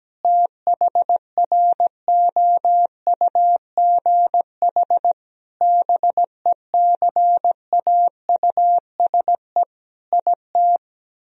Morse code, heard as a keyed tone, about 17 wpm, 700 Hz. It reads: THROUGH BECAUSE IT
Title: Sentences from Top 1000 Words - Review Part 1 - 17wpm